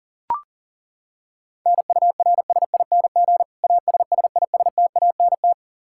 Morse code, heard as a keyed tone, about 40 wpm, 700 Hz. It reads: NURSING ASSISTANT
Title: Practice QSO - Exchange 3 - More Info - 40wpm